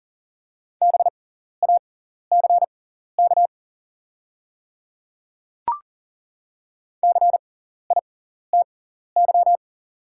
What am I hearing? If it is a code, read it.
Morse code, heard as a keyed tone, about 40 wpm, spaced as for 14 wpm, 700 Hz. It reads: BACK CITY
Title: 6x Character Spacing - Top 300 Words - 40wpm